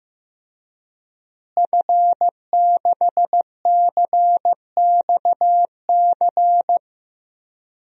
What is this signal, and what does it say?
Morse code, heard as a keyed tone, about 15 wpm, 700 Hz. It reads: F6CXC
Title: DXCC-100 Call Signs - 5 Characters - Repeated 2x - 15wpm